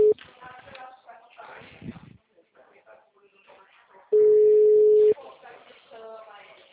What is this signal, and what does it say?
Ringback tone (Europe, 425 Hz), with speech.